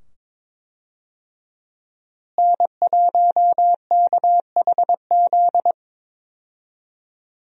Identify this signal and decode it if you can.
Morse code, heard as a keyed tone, about 22 wpm, 700 Hz. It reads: N1KHZ
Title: USA Call Signs - 5 Characters - Repeated 2x - 22wpm